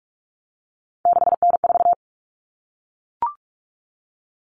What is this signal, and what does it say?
Morse code, heard as a keyed tone, about 44 wpm, 700 Hz. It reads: 6N4
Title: Common Cut Numbers - 3-Digit 45wpm